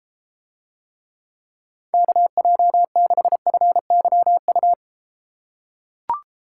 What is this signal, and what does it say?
Morse code, heard as a keyed tone, about 32 wpm, 700 Hz. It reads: KJ6FYU